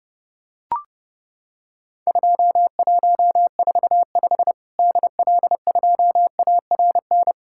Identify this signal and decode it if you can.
Morse code, heard as a keyed tone, about 32 wpm, 700 Hz. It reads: 2145 DL2ARN
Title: Worked All Europe DX Contest - QTC Exchange - 30wpm